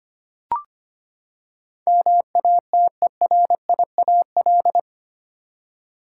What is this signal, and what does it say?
Morse code, heard as a keyed tone, about 25 wpm, 700 Hz. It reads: MATERIAL
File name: Eight Letter Words - 25wpm